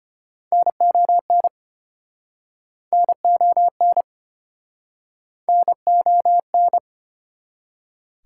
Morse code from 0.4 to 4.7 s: NON NON

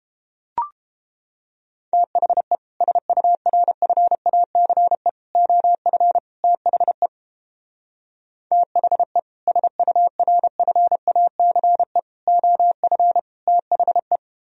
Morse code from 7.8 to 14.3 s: THE SURFACE OF THE

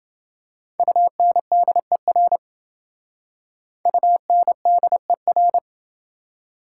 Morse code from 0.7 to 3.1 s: UNDER

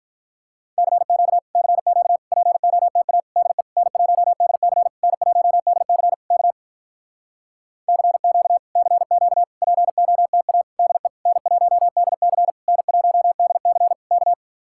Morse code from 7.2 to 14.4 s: CQ CQ POTA DE N1DC N1DC K

